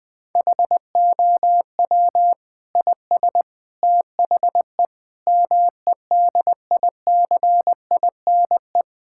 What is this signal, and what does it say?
Morse code, heard as a keyed tone, about 20 wpm, 700 Hz. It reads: HOW IS THE MEDICINE
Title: Sentences from Top 1000 Words - Review Part 6 - 20wpm